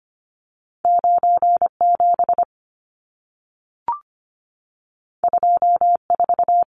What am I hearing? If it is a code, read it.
Morse code, heard as a keyed tone, about 25 wpm, 700 Hz. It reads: 97 24
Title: Numbers - 2-Digit 25wpm